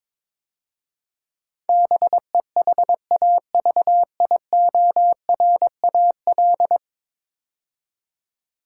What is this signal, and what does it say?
Morse code, heard as a keyed tone, about 22 wpm, 700 Hz. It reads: BEHAVIORAL
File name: practice / 22wpm / 10 Letter Words - First 3 Letters Given - 22wpm